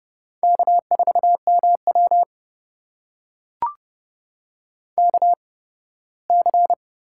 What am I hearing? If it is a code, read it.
Morse code, heard as a keyed tone, about 32 wpm, spaced as for 18 wpm, 700 Hz. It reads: K4MW KC
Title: USA Call Signs - 4 Characters - 8x Character Spacing - 30wpm